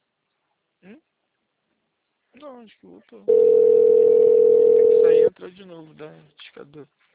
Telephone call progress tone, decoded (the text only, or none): ringback tone